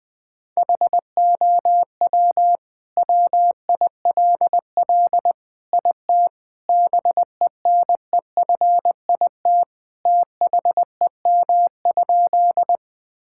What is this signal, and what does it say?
Morse code, heard as a keyed tone, about 20 wpm, 700 Hz. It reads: HOW WILL IT BENEFIT THEM?
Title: Sentences from Top 1000 Words - Review Part 6 - 20wpm